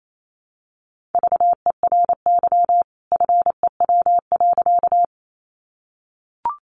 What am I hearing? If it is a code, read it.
Morse code, heard as a keyed tone, about 28 wpm, 700 Hz. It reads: VERY FEW.
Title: A Sound Of Thunder 28wpm